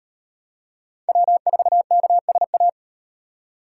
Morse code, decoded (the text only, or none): W4KSA